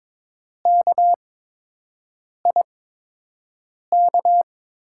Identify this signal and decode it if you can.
Morse code, heard as a keyed tone, about 23 wpm, spaced as for 6 wpm, 700 Hz. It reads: KIK
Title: USA Call Signs - 8x Character Spacing - 22wpm